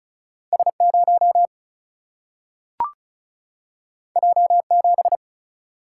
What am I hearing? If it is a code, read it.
Morse code, heard as a keyed tone, about 36 wpm, 700 Hz. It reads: S 0 J 7
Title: Single-Letter Single-Number 35wpm